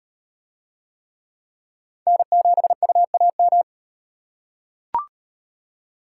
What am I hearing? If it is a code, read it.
Morse code, heard as a keyed tone, about 35 wpm, 700 Hz. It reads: N7UAM